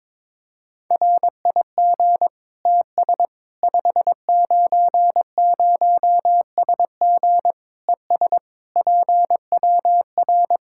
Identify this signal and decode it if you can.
Morse code, heard as a keyed tone, about 22 wpm, 700 Hz. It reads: RIG TS 590SG ES PWR
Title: K1USN SST QSO Practice 22wpm